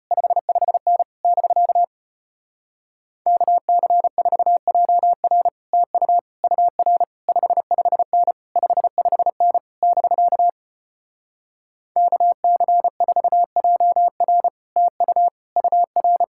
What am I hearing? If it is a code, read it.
Morse code, heard as a keyed tone, about 33 wpm, 700 Hz. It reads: 55N <BK> KC4JR TU UR 55N 55N <BK> KC4JR TU UR